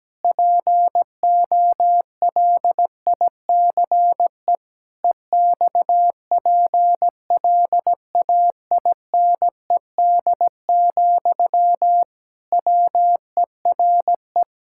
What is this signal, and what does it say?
Morse code, heard as a keyed tone, about 17 wpm, 700 Hz. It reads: POLICE EXPLAINED, WERE
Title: Pangram Challenge v2 - 17wpm